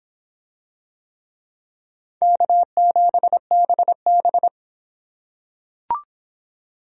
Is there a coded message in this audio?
Yes: Morse code from 2.2 to 4.5 s, heard as a keyed tone, about 24 wpm, 700 Hz. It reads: K7BB